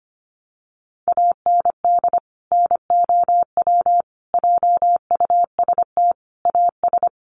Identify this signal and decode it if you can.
Morse code, heard as a keyed tone, about 25 wpm, 700 Hz. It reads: AND NOW JUST AS
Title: Sentences from Top 1200 Words 25wpm